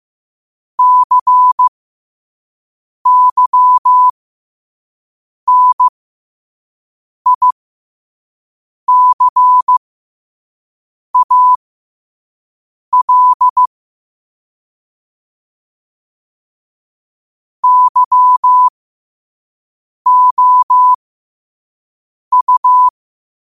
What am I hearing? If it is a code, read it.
Morse code, heard as a keyed tone, about 15 wpm, spaced as for 5 wpm, 1000 Hz. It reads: CYNICAL YOU
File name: qotdfiles / 5wpm